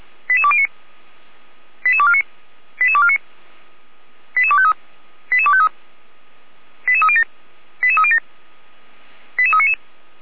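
Five-tone selective call (ZVEI), heard as quick runs of five tones, tones of about 70 ms. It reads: 88209 88249 88249 88253 88253 88307 88307 88399
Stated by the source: ZVEI five-tone selective call 88392, received with an SDR, archived by Feuerwehr Orlamünde.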